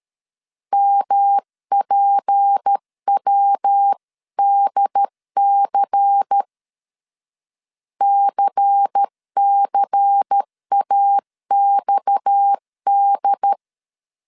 Morse code, heard as a keyed tone, about 13 wpm, 800 Hz. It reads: MPWDC CCAXD
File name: CW / DET / 1 6 1500C S C 12wpm 3533